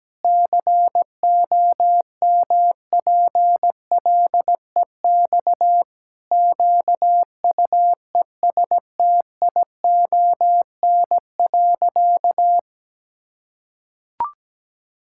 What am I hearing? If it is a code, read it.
Morse code, heard as a keyed tone, about 17 wpm, 700 Hz. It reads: COMPLEX QUESTION.
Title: Sentences from Top 1000 Words - Review Part 6 - 17wpm